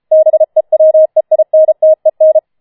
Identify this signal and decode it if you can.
Morse code, heard as a keyed tone, about 33 wpm, 630 Hz. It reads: BEWEINTEN
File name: Schneewittchen Teil5 34wpm